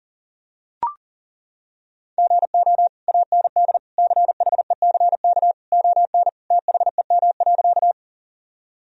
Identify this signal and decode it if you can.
Morse code, heard as a keyed tone, about 37 wpm, 700 Hz. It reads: GO AND CHECK ON THEM.
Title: Sentences from Top 1000 Words - Review Part 3 - 40wpm